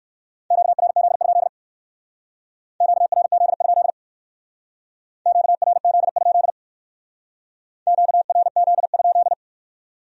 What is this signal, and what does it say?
Morse code, heard as a keyed tone, about 53 wpm, 700 Hz. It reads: QRZ? QRZ? QRZ? QRZ?
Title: Speed-Racing Q-Codes - 45wpm